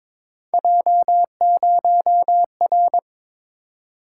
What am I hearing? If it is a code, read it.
Morse code, heard as a keyed tone, about 22 wpm, 700 Hz. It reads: J0R
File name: practice / 22wpm / Three Letter-Number Combinations - 22wpm